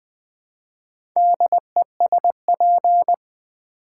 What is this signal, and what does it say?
Morse code, heard as a keyed tone, about 20 wpm, 700 Hz. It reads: DESP
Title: Build A Word - 4 Letters At A Time - 11-Letter Words - 20wpm